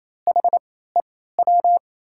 Morse code, heard as a keyed tone, about 28 wpm, spaced as for 16 wpm, 700 Hz. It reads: HEW
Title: USA Call Signs - 3x Character Spacing - 28wpm